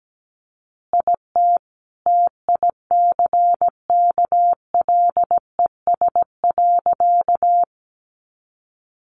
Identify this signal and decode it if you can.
Morse code, heard as a keyed tone, about 17 wpm, 700 Hz. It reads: IT TICKLES.
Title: Additional Easy Phrases 17wpm